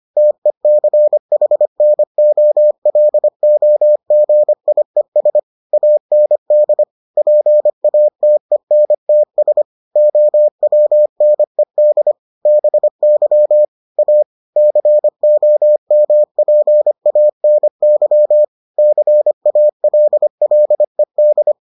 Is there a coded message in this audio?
Yes: Morse code from 0.2 to 21.5 s, heard as a keyed tone, about 25 wpm, 600 Hz. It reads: TECHNOLOGIES AND PATENTS OWNED BY A COMPANY CALLED